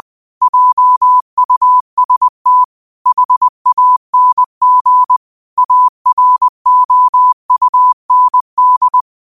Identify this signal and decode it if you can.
Morse code, heard as a keyed tone, about 20 wpm, 1000 Hz. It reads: JUST HANG AROUND
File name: qotdfiles / 20wpm